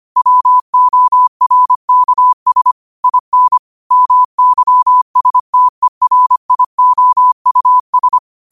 Morse code, heard as a keyed tone, about 25 wpm, 1000 Hz. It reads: WORKS IN MYSTERIOUS